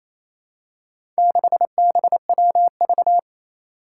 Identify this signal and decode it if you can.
Morse code, heard as a keyed tone, about 28 wpm, 700 Hz. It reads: 6BWV
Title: Four Challenging Letter-Number Combinations 28wpm